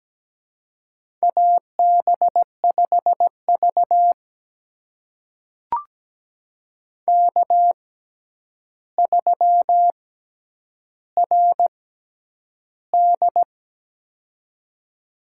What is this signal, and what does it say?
Morse code, heard as a keyed tone, about 17 wpm, spaced as for 9 wpm, 700 Hz. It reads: AB5V K3RD
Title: USA Call Signs - 4 Characters - 6x Character Spacing - 17wpm